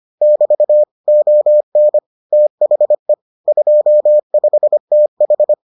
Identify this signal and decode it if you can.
Morse code, heard as a keyed tone, about 25 wpm, 600 Hz. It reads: <BT> ON THE 25TH